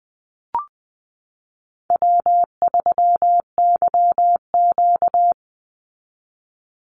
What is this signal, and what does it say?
Morse code, heard as a keyed tone, about 20 wpm, 700 Hz. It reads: W3YQ
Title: Easy Call Signs Repeated 2x - 20wpm